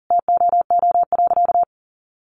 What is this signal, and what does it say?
Morse code, heard as a keyed tone, about 40 wpm, 700 Hz. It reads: TOO.
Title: Sentences from Top 1200 Words 40wpm